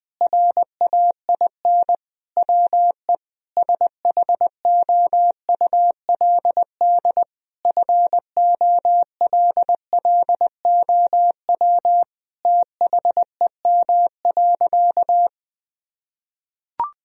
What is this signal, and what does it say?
Morse code, heard as a keyed tone, about 21 wpm, 700 Hz. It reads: RAIN WE SHOULD FOLLOW THEM.
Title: Sentences from Top 1000 Words - Review Part 1 - 20wpm